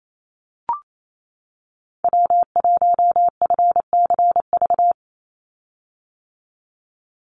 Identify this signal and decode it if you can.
Morse code, heard as a keyed tone, about 28 wpm, 700 Hz. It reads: W1FCV
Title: Call Signs - 28wpm